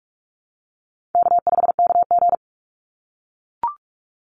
Morse code, heard as a keyed tone, about 44 wpm, 700 Hz. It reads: K5KG
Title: Easy Call Signs - 45wpm